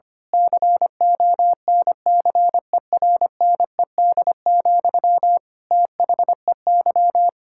Morse code, heard as a keyed tone, about 25 wpm, 700 Hz. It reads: CONCERNED, THEY
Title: Sentences from Top 1000 Words - Review Part 6 - 25wpm